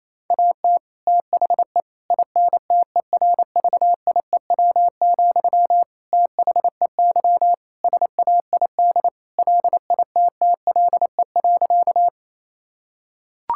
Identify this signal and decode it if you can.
Morse code, heard as a keyed tone, about 27 wpm, 700 Hz. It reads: AT THE INTERVIEW, THEY SAID LITTLE.
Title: Sentences from Top 1600 Words 28wpm